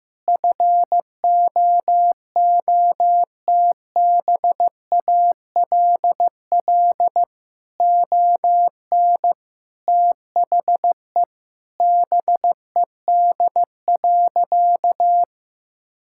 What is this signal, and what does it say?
Morse code, heard as a keyed tone, about 15 wpm, 700 Hz. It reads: FOOTBALL ON THE BED.
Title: Sentences from Top 1600 Words 15wpm